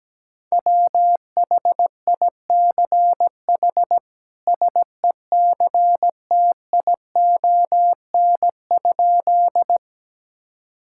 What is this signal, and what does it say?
Morse code, heard as a keyed tone, about 17 wpm, 700 Hz. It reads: WHICH SECTION?
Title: Sentences from Top 1000 Words - Review Part 5 - 17wpm